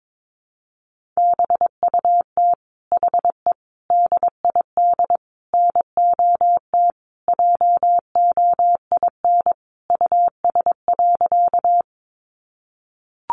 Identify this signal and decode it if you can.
Morse code, heard as a keyed tone, about 22 wpm, 700 Hz. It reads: BUT HE DID NOT JOIN US.
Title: Sentences from Top 1400 Words 22wpm